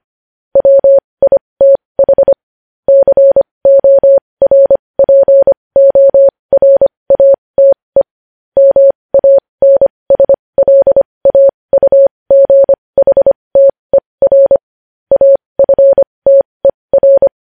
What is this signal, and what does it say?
Morse code, heard as a keyed tone, about 25 wpm, spaced as for 20 wpm, 550 Hz. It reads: WITH CORPORATE MANSLAUGHTER AFTER
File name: podcast / News headlines-1x-25x20